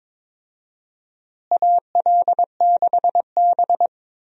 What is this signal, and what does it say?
Morse code, heard as a keyed tone, about 22 wpm, 700 Hz. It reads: AL6B
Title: USA Call Signs - 4 Characters - Repeated 3x - 22wpm